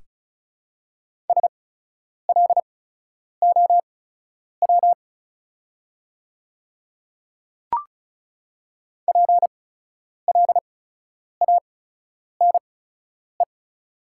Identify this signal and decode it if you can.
Morse code, heard as a keyed tone, about 36 wpm, spaced as for 10 wpm, 700 Hz. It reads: SLOW PLANE
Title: 8x Character Spacing - Top 500 Words - 35wpm